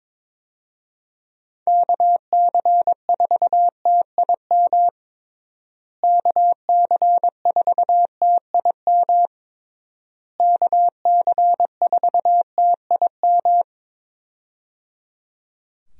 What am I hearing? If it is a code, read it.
Morse code, heard as a keyed tone, about 22 wpm, 700 Hz. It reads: KC4TIM KC4TIM KC4TIM
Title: USA Call Signs - 6 Characters - Repeated 3x - 22wpm